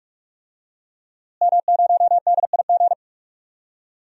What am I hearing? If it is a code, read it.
Morse code, heard as a keyed tone, about 46 wpm, 700 Hz. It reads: M0DIG
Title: DXCC-100 Call Signs - 5 Characters - Repeated 3x - 45wpm